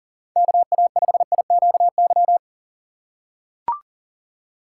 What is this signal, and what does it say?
Morse code, heard as a keyed tone, about 37 wpm, 700 Hz. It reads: KA5IQY
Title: USA Call Signs - 2x Character Spacing - 40wpm